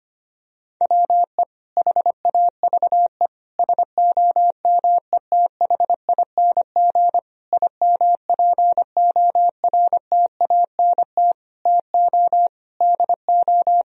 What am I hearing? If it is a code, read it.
Morse code, heard as a keyed tone, about 25 wpm, 700 Hz. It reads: WE HAVE SOMETHING IMPORTANT TO DO
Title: Sentences from Top 1000 Words - Review Part 4 - 25wpm